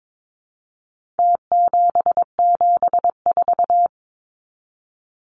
Morse code, heard as a keyed tone, about 22 wpm, 700 Hz. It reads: T774
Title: Numbers - Contest Style - 4-Digit 22wpm